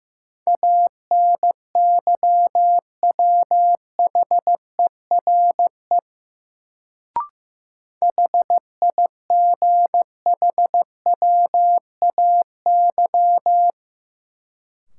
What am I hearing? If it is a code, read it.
Morse code, heard as a keyed tone, about 15 wpm, 700 Hz. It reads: ANYWHERE HIGHWAY
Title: Sentences from Top 1400 Words - New Word Drill Alt - 15wpm